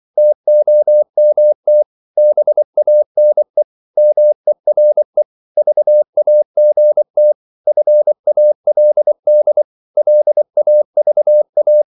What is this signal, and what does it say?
Morse code, heard as a keyed tone, about 24 wpm, 600 Hz. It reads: TOMT BANE MERE VAGT FALD LAVA
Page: Korte danske ord 24wpm